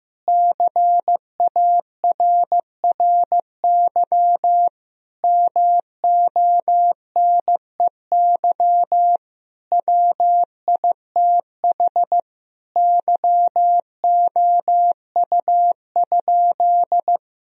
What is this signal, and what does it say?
Morse code, heard as a keyed tone, about 15 wpm, 700 Hz. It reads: CARRY MONEY WITH YOU?
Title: Sentences from Top 1000 Words - Review Part 1 - 15wpm